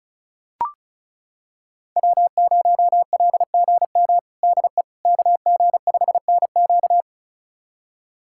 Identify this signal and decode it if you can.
Morse code, heard as a keyed tone, about 34 wpm, 700 Hz. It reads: W0LGM DE KG5NQ
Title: USA Callsign DE Callsign - 5 Characters - 35wpm